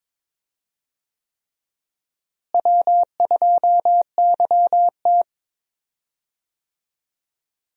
Morse code, heard as a keyed tone, about 22 wpm, 700 Hz. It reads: W2YT
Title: USA Call Signs - 4 Characters - Repeated 3x - 22wpm